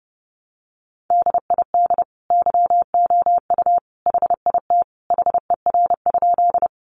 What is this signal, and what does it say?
Morse code, heard as a keyed tone, about 30 wpm, 700 Hz. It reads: DID YOU HIT HER?
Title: Sentences from Top 1200 Words 30wpm